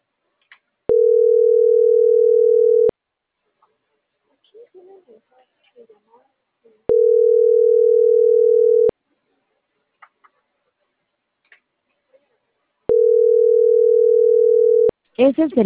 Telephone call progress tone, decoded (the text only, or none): ringback tone